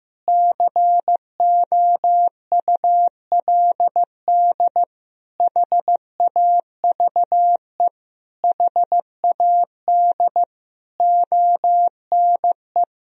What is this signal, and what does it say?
Morse code, heard as a keyed tone, about 15 wpm, 700 Hz. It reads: COULD HAVE HAD ONE
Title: Sentences from Top 1000 Words - Review Part 3 - 15wpm